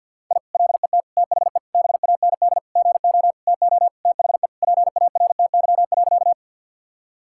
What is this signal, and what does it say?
Morse code, heard as a keyed tone, about 50 wpm, 700 Hz. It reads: I LET THE BAND GO TO THE PARTY.
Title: Sentences from Top 1600 Words 50wpm